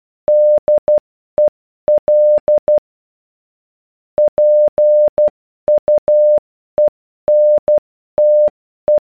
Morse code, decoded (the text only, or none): DEL PUENTE